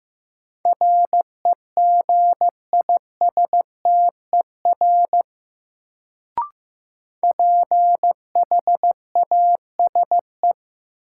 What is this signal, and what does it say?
Morse code, heard as a keyed tone, about 15 wpm, 700 Hz. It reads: REGISTER PHASE
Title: Sentences from Top 1200 Words - New Word Drill Alt - 15wpm